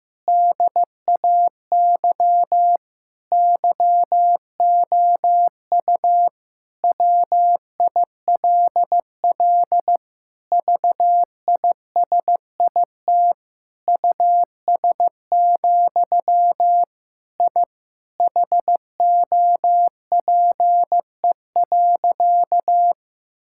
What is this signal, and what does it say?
Morse code, heard as a keyed tone, about 15 wpm, 700 Hz. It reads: DAY YOU WILL VISIT US, I HOPE.
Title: Sentences from Top 1000 Words - Review Part 6 - 15wpm